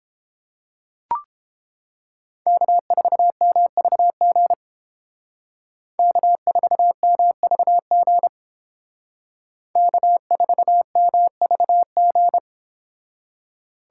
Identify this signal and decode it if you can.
Morse code, heard as a keyed tone, about 31 wpm, 700 Hz. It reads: K4MVG K4MVG K4MVG